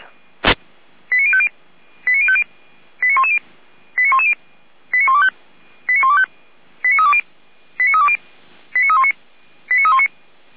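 Five-tone selective call (ZVEI), heard as quick runs of five tones, tones of about 70 ms. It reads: 89050 89050 89110 89110 89125 89125 89320 89320 89319 89319
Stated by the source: ZVEI five-tone selective call received on SDR.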